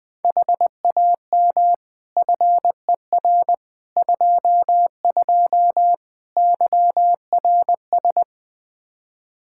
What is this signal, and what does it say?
Morse code, heard as a keyed tone, about 20 wpm, 700 Hz. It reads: HAM FER 22 YRS
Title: Practice QSO - Exchange 3 - More Info - 20wpm